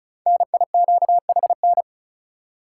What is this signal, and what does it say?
Morse code, heard as a keyed tone, about 36 wpm, 700 Hz. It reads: NIQHN
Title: Code Groups - 5 Letter:Numbers - 35wpm